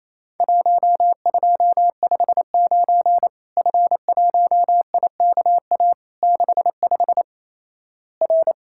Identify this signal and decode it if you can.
Morse code, heard as a keyed tone, about 27 wpm, 700 Hz. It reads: 1259 F1IKA 65 R
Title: Worked All Europe DX Contest - QTC Exchange - 28wpm